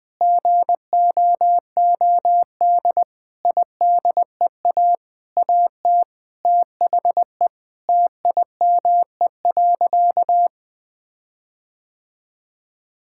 Morse code, heard as a keyed tone, about 20 wpm, 700 Hz. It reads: GOOD IDEA AT THE TIME.
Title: Sentences with 8-Word Phrases - 20wpm